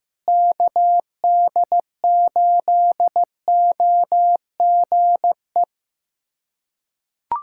Morse code, heard as a keyed tone, about 14 wpm, 700 Hz. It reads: KD8OGE
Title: USA Callsign DE Callsign - 6 Characters - 15wpm